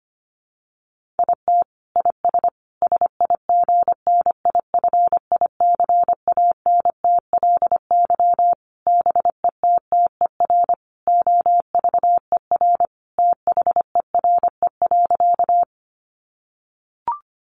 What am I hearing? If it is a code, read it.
Morse code, heard as a keyed tone, about 25 wpm, 700 Hz. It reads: IT IS SIGNIFICANTLY BETTER OVER THERE.
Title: Sentences from Top 1500 Words 25wpm